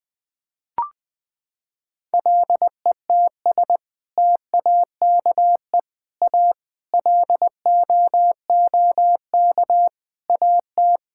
Morse code, heard as a keyed tone, about 20 wpm, 700 Hz. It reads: LETS TAKE A LOOK AT
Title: Sets of 6 Words - 20wpm